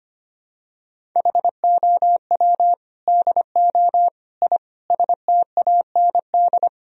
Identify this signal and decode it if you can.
Morse code, heard as a keyed tone, about 25 wpm, 700 Hz. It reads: HOW DO I STAND
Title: Sentences from Top 1000 Words - Review Part 1 - 25wpm